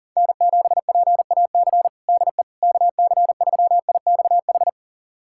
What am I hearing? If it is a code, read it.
Morse code, heard as a keyed tone, about 37 wpm, 700 Hz. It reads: N7PAC DE KC3IXH
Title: USA Callsign DE Callsign 40wpm